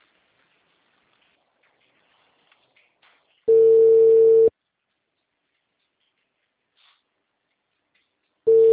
Ringback tone (China, 450 Hz).